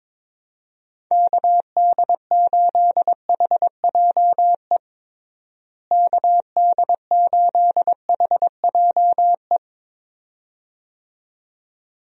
Morse code, heard as a keyed tone, about 22 wpm, 700 Hz. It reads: KD8HJE KD8HJE
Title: USA Call Signs - 6 Characters - Repeated 2x - 22wpm